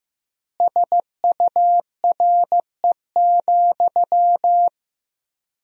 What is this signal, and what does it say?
Morse code, heard as a keyed tone, about 15 wpm, 700 Hz. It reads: SURE,
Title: 3x Word-Spacing - Sentences from Top 500 Words - 15wpm